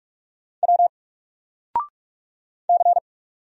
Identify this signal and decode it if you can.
Morse code, heard as a keyed tone, about 46 wpm, 700 Hz. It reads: W C
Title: Single Letter-Number 45wpm